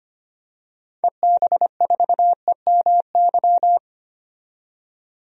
Morse code, heard as a keyed tone, about 25 wpm, 700 Hz. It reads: EB4EMY
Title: DXCC-100 Call Signs - 6 Characters - 25wpm